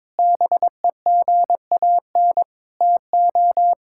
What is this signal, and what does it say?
Morse code, heard as a keyed tone, about 22 wpm, 700 Hz. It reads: BEGAN TO
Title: Sentences from Top 1000 Words - Review Part 5 - 22wpm